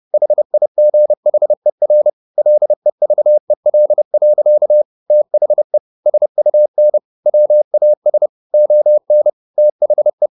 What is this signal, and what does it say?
Morse code, heard as a keyed tone, about 30 wpm, 600 Hz. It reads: HIGHER LEVEL. THE SUN WAS ON THE
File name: the old man and the sea / the old man and the sea 1 30wpm